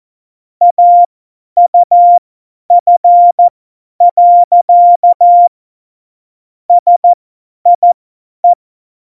Morse code, heard as a keyed tone, about 14 wpm, spaced as for 10 wpm, 700 Hz. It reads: AUF. SIE
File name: Morsekurs / Witze / Witz 14 Juli 2020, 50 F70